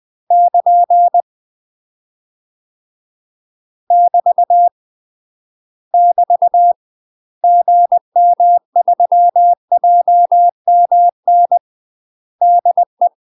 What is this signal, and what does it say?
Morse code, heard as a keyed tone, about 20 wpm, 700 Hz. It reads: <KN> <BT> <BT> GM3JMN DE
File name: UK MASTER QSOs TRAINING / 20WPM / UK MASTER QSOs 0013